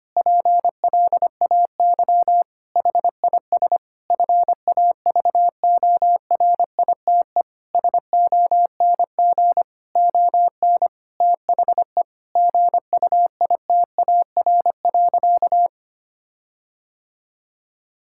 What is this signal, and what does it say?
Morse code, heard as a keyed tone, about 25 wpm, 700 Hz. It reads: PLAY HIS FAVORITE SONG ON THE GUITAR.
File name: practice / 25wpm / Difficult Sentences with Hints - 25wpm